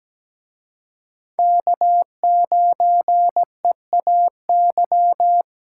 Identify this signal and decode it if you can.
Morse code, heard as a keyed tone, about 20 wpm, 700 Hz. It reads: K9EAY
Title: Speed Racing - USA 1x3 Call Signs 17wpm